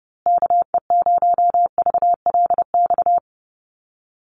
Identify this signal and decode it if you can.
Morse code, heard as a keyed tone, about 30 wpm, 700 Hz. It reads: KE0VLX
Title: USA Call Signs Repeated 3x 30wpm